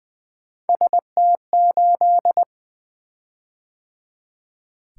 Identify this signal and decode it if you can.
Morse code, heard as a keyed tone, about 20 wpm, 700 Hz. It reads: ST8
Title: Three Letter-Number Combinations - 20wpm